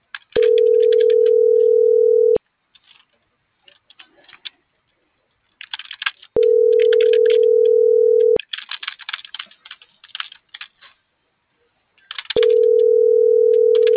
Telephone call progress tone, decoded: ringback tone

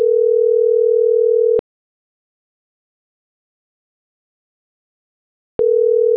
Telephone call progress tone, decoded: ringback tone

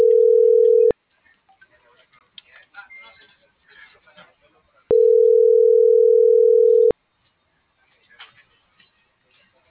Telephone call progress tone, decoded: ringback tone

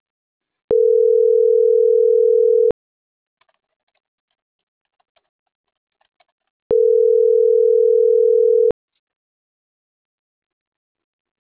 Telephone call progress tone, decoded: ringback tone

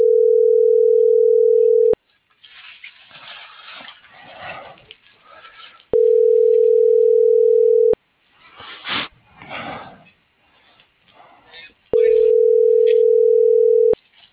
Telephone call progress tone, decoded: ringback tone

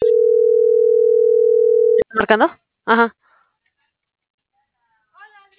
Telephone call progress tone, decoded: ringback tone

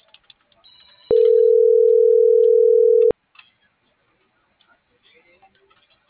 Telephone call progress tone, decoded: ringback tone

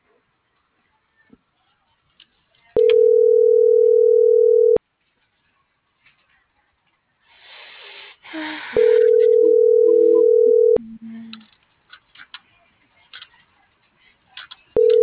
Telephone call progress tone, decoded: ringback tone

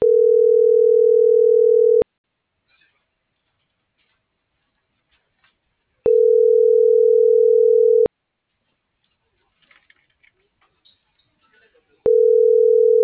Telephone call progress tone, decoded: ringback tone

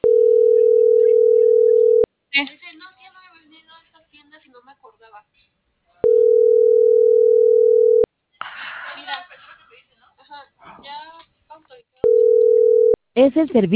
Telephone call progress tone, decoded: ringback tone

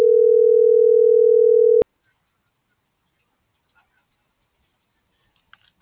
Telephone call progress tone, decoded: ringback tone

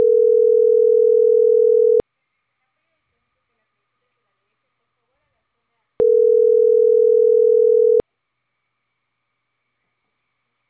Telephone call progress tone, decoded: ringback tone

